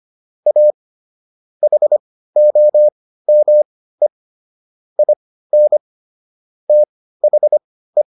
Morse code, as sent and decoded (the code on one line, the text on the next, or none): .- / .... --- -- . / .. -. / - .... .
A HOME IN THE